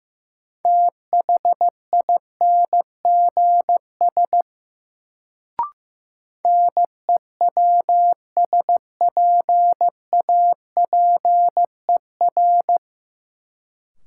Morse code, heard as a keyed tone, about 15 wpm, 700 Hz. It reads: THINGS NEWSPAPER
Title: Sentences from Top 1000 Words - Review Part 4 - New Word Drill - 15wpm